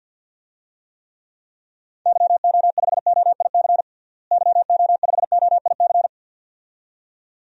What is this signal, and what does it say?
Morse code, heard as a keyed tone, about 46 wpm, 700 Hz. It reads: YO5OIC YO5OIC
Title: DXCC-100 Call Signs - 6 Characters - Repeated 2x - 50wpm